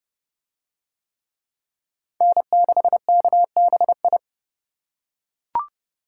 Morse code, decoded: N6KBI

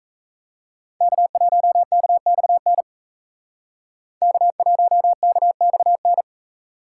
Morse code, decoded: K1KXN K1KXN

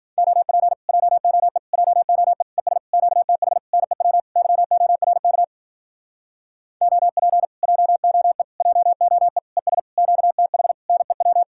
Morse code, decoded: OP JOE JOE ES QTH NEW YORK OP JOE JOE ES QTH NEW

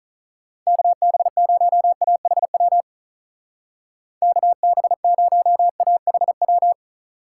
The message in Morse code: KB0AHW KB0AHW

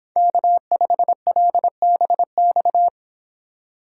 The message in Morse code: K5LBX

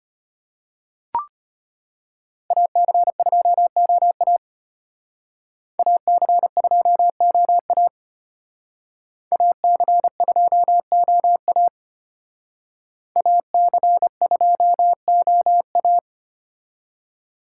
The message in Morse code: AC2OA AC2OA AC2OA AC2OA

